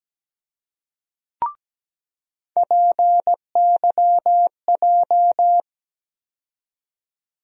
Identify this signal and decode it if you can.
Morse code, heard as a keyed tone, about 17 wpm, 700 Hz. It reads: PYJ
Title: USA Call Sign Postfixes - 3 Letters 17wpm